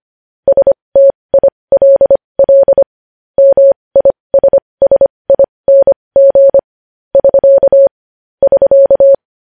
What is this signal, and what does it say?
Morse code, heard as a keyed tone, about 25 wpm, spaced as for 20 wpm, 550 Hz. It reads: STILL MISSING <SK> <SK>
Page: news-headlines-in-morse-code-at-20-wpm